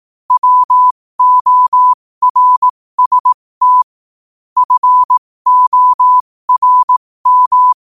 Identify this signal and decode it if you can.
Morse code, heard as a keyed tone, about 18 wpm, spaced as for 15 wpm, 1000 Hz. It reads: WORST FORM
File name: qotdfiles / 15wpm